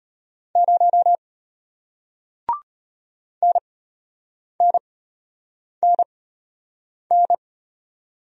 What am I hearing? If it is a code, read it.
Morse code, heard as a keyed tone, about 33 wpm, spaced as for 8 wpm, 700 Hz. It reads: 0 NNNN